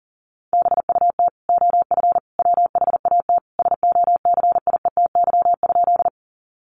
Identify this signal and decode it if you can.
Morse code, heard as a keyed tone, about 40 wpm, 700 Hz. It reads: BUT OF WHAT SOCIETY?
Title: Sentences from Top 700 Words 40wpm